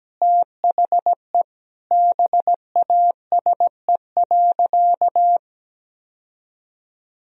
Morse code, heard as a keyed tone, about 17 wpm, 700 Hz. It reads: THE BASE.
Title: Sentences from Top 1000 Words - Review Part 1 - 17wpm